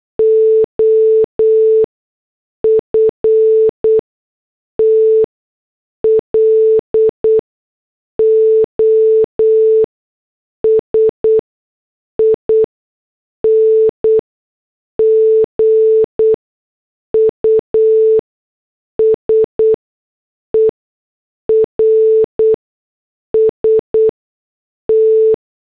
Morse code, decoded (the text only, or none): OFT LOSING USERS T